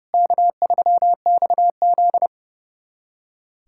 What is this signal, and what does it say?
Morse code, heard as a keyed tone, about 30 wpm, 700 Hz. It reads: K3XZ
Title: USA Callsign DE Callsign - 4 Characters - 30wpm